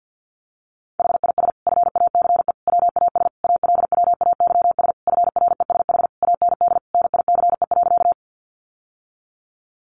Morse code, heard as a keyed tone, about 50 wpm, 700 Hz. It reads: HIS FACE WAS ALWAYS FRESH AND NICE.
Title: Sentences from Top 1200 Words 50wpm